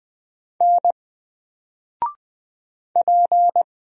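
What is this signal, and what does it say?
Morse code, heard as a keyed tone, about 20 wpm, 700 Hz. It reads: N P
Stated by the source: Single letter and number Morse drill at 20 wpm, from N4LCD.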